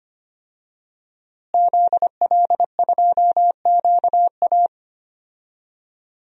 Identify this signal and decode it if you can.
Morse code, heard as a keyed tone, about 25 wpm, 700 Hz. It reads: ZL2QA
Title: DXCC-100 Call Signs - 5 Characters - Repeated 3x - 25wpm